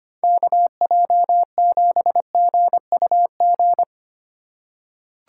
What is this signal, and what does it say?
Morse code, heard as a keyed tone, about 25 wpm, 700 Hz. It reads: KJ7GUG
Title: USA Call Signs - 6 Characters - 3x Character Spacing - 25wpm